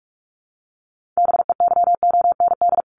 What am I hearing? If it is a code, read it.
Morse code, heard as a keyed tone, about 44 wpm, 700 Hz. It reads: BEYOND
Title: Sentences from Top 1500 Words 45wpm